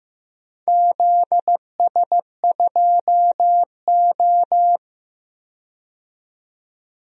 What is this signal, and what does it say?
Morse code, heard as a keyed tone, about 15 wpm, 700 Hz. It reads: ZS2O
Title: DXCC-100 Call Signs - 4 Characters - 15wpm